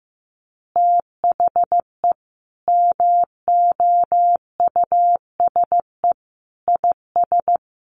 Morse code, heard as a keyed tone, about 15 wpm, 700 Hz. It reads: THE MOUSE IS
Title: Easy Phrases 15wpm